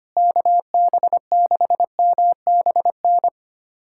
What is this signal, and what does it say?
Morse code, heard as a keyed tone, about 25 wpm, 700 Hz. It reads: KB6MBN
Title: USA Callsign DE Callsign 25wpm